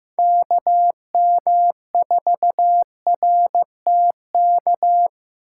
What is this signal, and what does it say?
Morse code, heard as a keyed tone, about 15 wpm, 700 Hz. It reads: KM4RTK
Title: USA Call Signs - 6 Characters - 4x Character Spacing - 15wpm